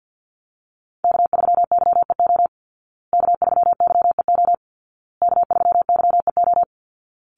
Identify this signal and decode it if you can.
Morse code, heard as a keyed tone, about 49 wpm, 700 Hz. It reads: K3YEO K3YEO K3YEO
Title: Call Signs Repeated 2x - 50wpm